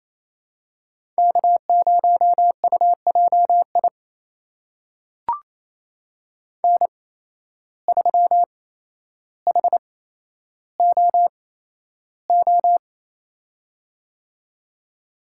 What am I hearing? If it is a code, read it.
Morse code, heard as a keyed tone, about 28 wpm, spaced as for 12 wpm, 700 Hz. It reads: K0UJI N3HOO
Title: USA Call Signs - 5 Characters - 8x Character Spacing - 28wpm